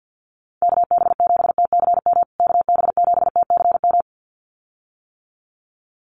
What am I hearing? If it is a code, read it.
Morse code, heard as a keyed tone, about 50 wpm, 700 Hz. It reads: KB7TCM KB7TCM
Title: USA Call Signs Repeated 2x 50wpm